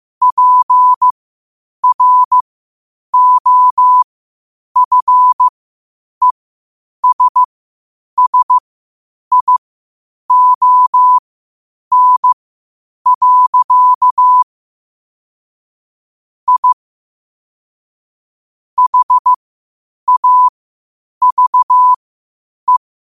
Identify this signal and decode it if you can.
Morse code, heard as a keyed tone, about 15 wpm, spaced as for 8 wpm, 1000 Hz. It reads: PROFESSION. I HAVE
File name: qotdfiles / 8wpm